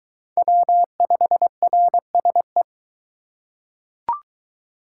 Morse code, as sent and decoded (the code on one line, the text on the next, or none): .-- ..... .-. ... .
W5RSE